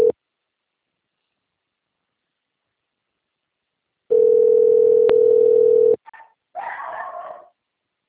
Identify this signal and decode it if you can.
Ringback tone (North America), 440 + 480 Hz.